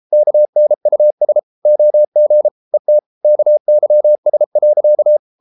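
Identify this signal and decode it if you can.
Morse code, heard as a keyed tone, about 33 wpm, 600 Hz. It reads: KNUS OG ET KYS.
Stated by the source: Morse code at 33 wpm, QRQ.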